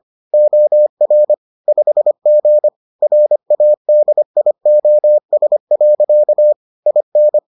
Morse code, heard as a keyed tone, about 25 wpm, 600 Hz. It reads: OR 5G RADIOS. IN